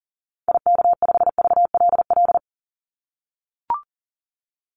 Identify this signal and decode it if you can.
Morse code, heard as a keyed tone, about 40 wpm, 700 Hz. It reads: IK5VLL
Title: Call Signs - 40wpm